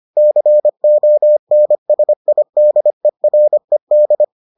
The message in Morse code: CONSIDERED

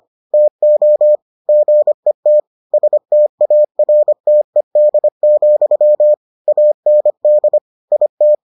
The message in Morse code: TO GET STARTED, AND IT

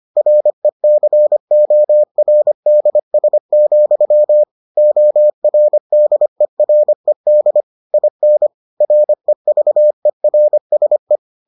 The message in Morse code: RECORDS, ORDERED IN REVERSE

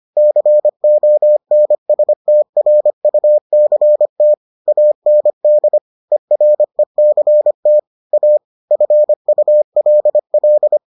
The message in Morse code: CONSTRUCT AND ERECT A FULL

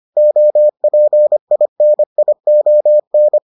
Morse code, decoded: OPINION

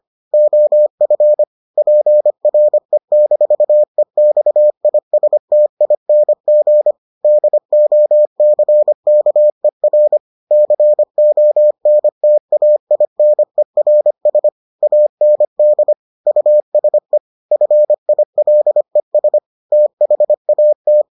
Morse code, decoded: OF PRE-EXISTING DOCKER CONTAINERS AND USE FILES THAT

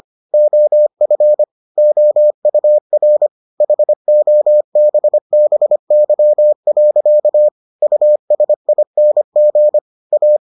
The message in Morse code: OF OUR HOBBY. USING A

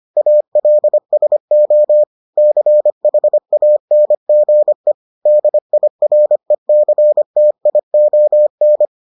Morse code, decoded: ALSO CHANGE DIRECTION

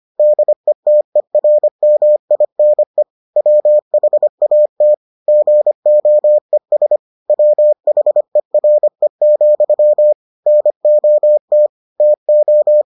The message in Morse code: DETERMINE WHAT GOES WHERE, NOT TO

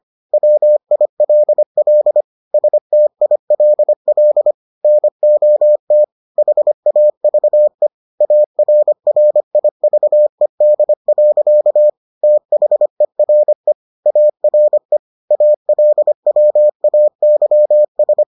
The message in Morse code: WILL STILL NOT HAVE ARRIVED. THERE ARE ALWAYS

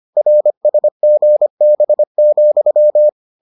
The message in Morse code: RSGB,

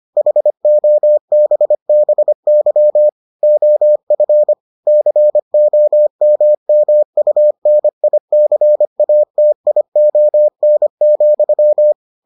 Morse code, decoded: HOBBY OF COMMUNICATION,

